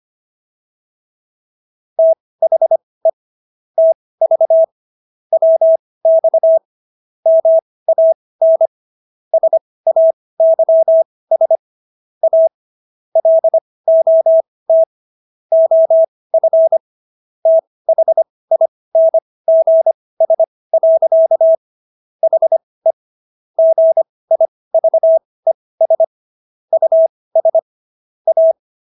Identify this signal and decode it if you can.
Morse code, heard as a keyed tone, about 25 wpm, spaced as for 18 wpm, 650 Hz. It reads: THE TV WX MAN SAYS A LOT OF THINGS. HE GIVES US A